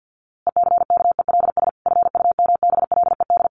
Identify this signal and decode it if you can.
Morse code, heard as a keyed tone, about 50 wpm, 700 Hz. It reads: ECKELS FUMBLED
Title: A Sound Of Thunder 50wpm